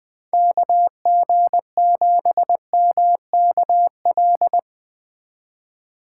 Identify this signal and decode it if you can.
Morse code, heard as a keyed tone, about 20 wpm, 700 Hz. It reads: KG7MKL
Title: Speed Racing - USA Call Signs 17wpm